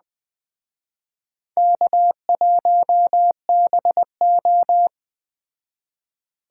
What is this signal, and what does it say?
Morse code, heard as a keyed tone, about 20 wpm, 700 Hz. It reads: K1BO
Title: Speed Racing - USA Call Signs 17wpm